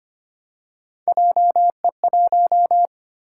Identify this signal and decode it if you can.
Morse code, heard as a keyed tone, about 25 wpm, 700 Hz. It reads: JE1
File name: Three Letter-Number Combinations - 25wpm